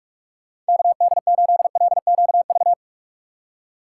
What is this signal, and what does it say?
Morse code, heard as a keyed tone, about 41 wpm, 700 Hz. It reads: KD8LQV